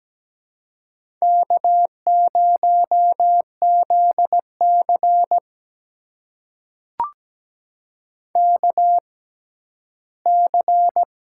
Morse code, heard as a keyed tone, about 17 wpm, spaced as for 12 wpm, 700 Hz. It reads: K0ZC KC
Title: USA Call Signs - 6x Character Spacing - 17wpm